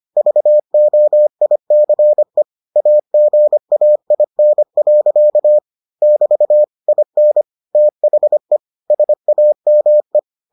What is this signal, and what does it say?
Morse code, heard as a keyed tone, about 25 wpm, 600 Hz. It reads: VOICE AGAIN. <BT> IN THE SAME